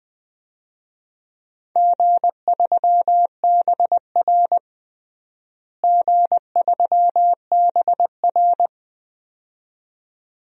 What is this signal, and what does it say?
Morse code, heard as a keyed tone, about 21 wpm, 700 Hz. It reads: G3BR G3BR
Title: DXCC-100 Call Signs - 4 Characters - Repeated 2x - 20wpm